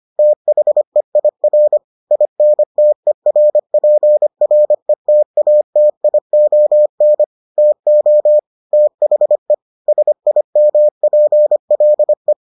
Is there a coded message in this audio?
Yes: Morse code from 0.2 to 12.3 s, heard as a keyed tone, about 25 wpm, 600 Hz. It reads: THEIR INTERPRETATION TO THE SIMPLE